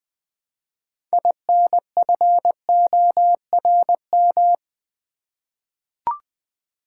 Morse code, heard as a keyed tone, about 20 wpm, 700 Hz. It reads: INFORM
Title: 6 Letter Words - First 3 Letters Given - 20wpm